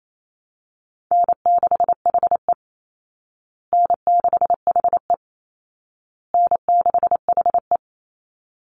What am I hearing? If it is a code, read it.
Morse code, heard as a keyed tone, about 28 wpm, 700 Hz. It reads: N6HE N6HE N6HE
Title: Easy Call Signs Repeated 3x - 28wpm